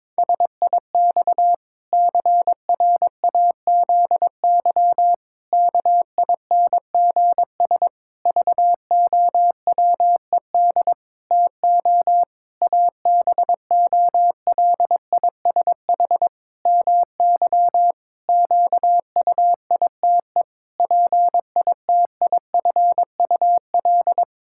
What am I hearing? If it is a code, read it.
Morse code, heard as a keyed tone, about 22 wpm, 700 Hz. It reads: SIX CRAZY KINGS VOWED TO ABOLISH MY QUITE PITIFUL
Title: Pangram Challenge v2 - 22wpm